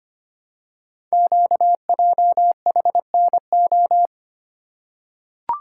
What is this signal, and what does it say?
Morse code, heard as a keyed tone, about 25 wpm, 700 Hz. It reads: QJHNO
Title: Code Groups - 5 Letters - 25wpm